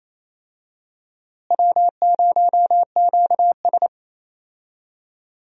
Morse code, heard as a keyed tone, about 28 wpm, 700 Hz. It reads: W0QS
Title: USA Call Signs - 4 Characters - Repeated 3x - 28wpm